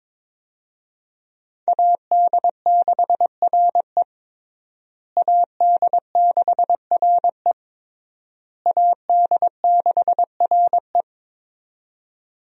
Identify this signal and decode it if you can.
Morse code, heard as a keyed tone, about 22 wpm, 700 Hz. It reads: AD6RE AD6RE AD6RE
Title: USA Call Signs - 5 Characters - Repeated 3x - 22wpm